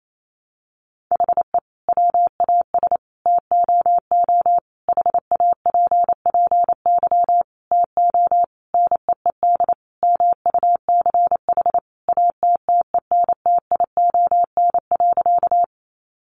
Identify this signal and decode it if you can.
Morse code, heard as a keyed tone, about 28 wpm, 700 Hz. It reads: HE WAS TOO HAPPY TO NEED MUCH ATTENTION.
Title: Sentences from Top 1000 Words 28wpm